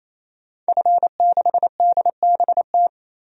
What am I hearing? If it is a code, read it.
Morse code, heard as a keyed tone, about 27 wpm, 700 Hz. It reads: F6DBT
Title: DXCC-100 Call Signs - 5 Characters - 28wpm